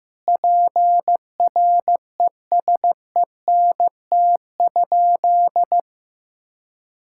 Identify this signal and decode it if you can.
Morse code, heard as a keyed tone, about 15 wpm, 700 Hz. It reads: PRESENT?
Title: Sentences from Top 1000 Words - Review Part 6 - 15wpm